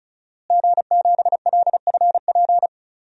Morse code, heard as a keyed tone, about 36 wpm, 700 Hz. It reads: G7LFP
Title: DXCC-100 Call Signs - 5 Characters - Repeated 3x - 35wpm